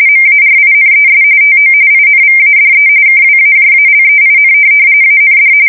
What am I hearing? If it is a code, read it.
RTTY, about 45.45 baud. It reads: MARINES TO INVADE KWAJALEIN AND
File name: fsk170 / USS-New-Jersey